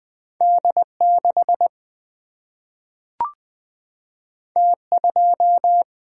Morse code, heard as a keyed tone, about 20 wpm, 700 Hz. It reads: D6 T2
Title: Single-Letter Single-Number 20wpm